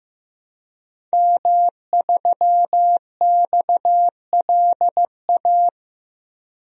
Morse code, heard as a keyed tone, about 15 wpm, 700 Hz. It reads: M3XLA
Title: DXCC-100 Call Signs - 5 Characters - 15wpm